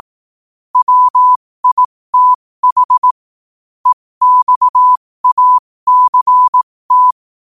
Morse code, heard as a keyed tone, about 18 wpm, spaced as for 15 wpm, 1000 Hz. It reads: WITH EXACT
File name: qotdfiles / 15wpm